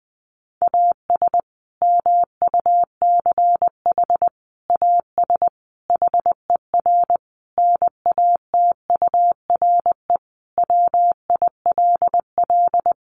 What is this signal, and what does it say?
Morse code, heard as a keyed tone, about 20 wpm, 700 Hz. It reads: AS MUCH AS HER NATURE WILL
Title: Sentences from Top 700 Words 20wpm